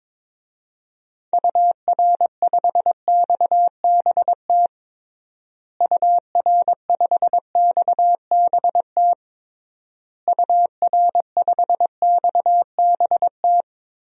Morse code, heard as a keyed tone, about 22 wpm, 700 Hz. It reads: UR5XBT UR5XBT UR5XBT
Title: DXCC-100 Call Signs - 6 Characters - Repeated 3x - 22wpm